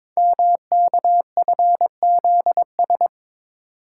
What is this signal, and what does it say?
Morse code, heard as a keyed tone, about 22 wpm, 700 Hz. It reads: MKFZS
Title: Code Groups - 5 Letters - 22wpm